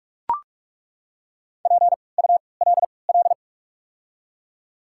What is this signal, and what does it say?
Morse code, heard as a keyed tone, about 42 wpm, spaced as for 25 wpm, 690 Hz. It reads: PULL
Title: 3x Character Spacing - Top 500 Words - 45wpm